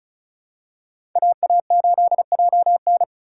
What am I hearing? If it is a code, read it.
Morse code, heard as a keyed tone, about 36 wpm, 700 Hz. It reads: AA8JN
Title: USA Call Signs - 5 Characters - 6x Character Spacing - 35wpm